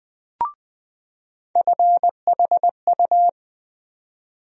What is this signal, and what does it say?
Morse code, heard as a keyed tone, about 20 wpm, 700 Hz. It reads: FHU
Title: USA Call Sign Postfixes - 3 Letters 20wpm